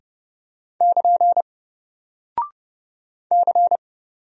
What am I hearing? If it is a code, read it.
Morse code, heard as a keyed tone, about 31 wpm, 700 Hz. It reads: <KN> C
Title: Single Character (Letter:Number:Punctuation:Prosign) 30wpm